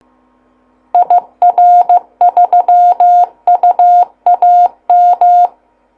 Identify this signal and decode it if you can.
Morse code, heard as a keyed tone, about 15 wpm, 710 Hz. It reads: IR3UAM